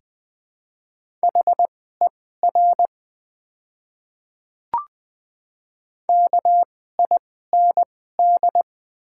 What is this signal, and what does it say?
Morse code, heard as a keyed tone, about 20 wpm, spaced as for 14 wpm, 700 Hz. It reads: HER KIND
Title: 2x Character Spacing - Top 200 Words - 20wpm